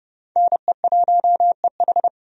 Morse code, heard as a keyed tone, about 30 wpm, 700 Hz. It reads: NE1EH
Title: USA Call Signs - 5 Characters - 30wpm